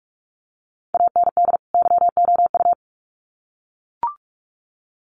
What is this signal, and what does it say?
Morse code, heard as a keyed tone, about 44 wpm, 700 Hz. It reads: AND YOU
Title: Sets of 2 Words 45wpm